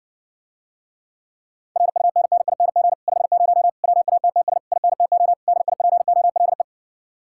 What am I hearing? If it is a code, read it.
Morse code, heard as a keyed tone, about 60 wpm, 700 Hz. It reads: RUNNING 50 WATTS INTO DIPOLE